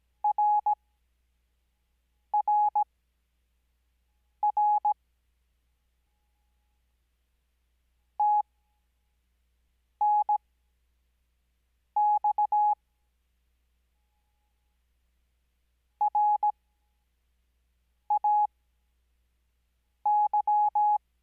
Morse code, heard as a keyed tone, about 17 wpm, spaced as for 5 wpm, 840 Hz. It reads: RRR TNX RAY
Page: Farnsworth Spacing: 15WPM Character speed Spaced at 5WPM.